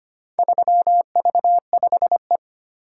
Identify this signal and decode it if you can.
Morse code, heard as a keyed tone, about 25 wpm, 700 Hz. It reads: 3V5E
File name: practice / 25wpm / Four Letter-Number Combinations - 25wpm